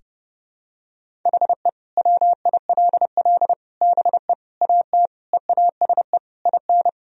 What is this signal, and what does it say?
Morse code, heard as a keyed tone, about 30 wpm, 700 Hz. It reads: HE WILL BE AT EASE IN
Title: Sentences from Top 1000 Words - Review Part 1 - 30wpm